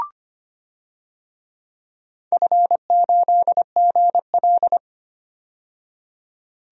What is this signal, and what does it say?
Morse code, heard as a keyed tone, about 25 wpm, 700 Hz. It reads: F8GL
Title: DXCC-100 Call Signs - 4 Characters - 25wpm